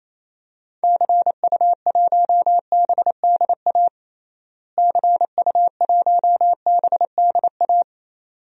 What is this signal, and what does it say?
Morse code, heard as a keyed tone, about 28 wpm, 700 Hz. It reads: CU1BDA CU1BDA
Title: DXCC-100 Call Signs - 6 Characters - Repeated 2x - 28wpm